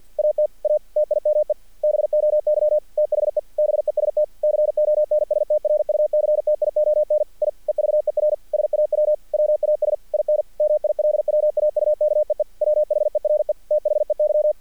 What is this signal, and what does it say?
Morse code, heard as a keyed tone, about 48 wpm, 600 Hz. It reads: AT A TIME BOY THE BEST CONSTRUCTION I EVER SAW WAS IN MILWAUKEE WHERE THEY
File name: archived / CRSnet